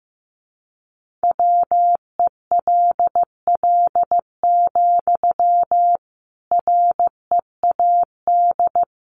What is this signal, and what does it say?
Morse code, heard as a keyed tone, about 15 wpm, 700 Hz. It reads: WELL, READ
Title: Sentences from Top 300 Words 15wpm